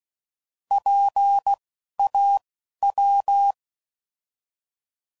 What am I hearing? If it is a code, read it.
Morse code, heard as a keyed tone, about 16 wpm, spaced as for 11 wpm, 800 Hz. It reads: PAW
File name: Prefixes / By First Letter / Ps